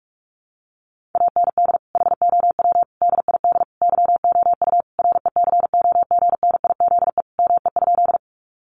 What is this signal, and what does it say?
Morse code, heard as a keyed tone, about 44 wpm, 700 Hz. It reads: AND HOW DID YOU RECOGNIZE ME?
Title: Sentences from Top 1400 Words 45wpm